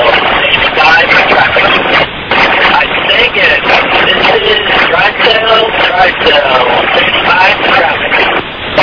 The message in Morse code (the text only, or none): none